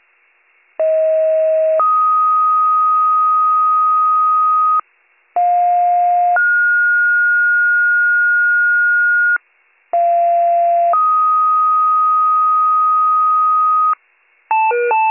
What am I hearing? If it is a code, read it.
Two-tone page, heard as a one-second tone, then a three-second tone, about 640 then 1250 Hz, about 710 then 1430 Hz, about 690 then 1250 Hz. It is heard from 0.8 to 13.9 s.